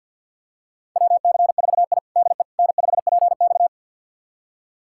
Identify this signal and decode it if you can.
Morse code, heard as a keyed tone, about 51 wpm, 700 Hz. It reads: WC4I DE N5PX
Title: USA Callsign DE Callsign - 4 Characters - 50wpm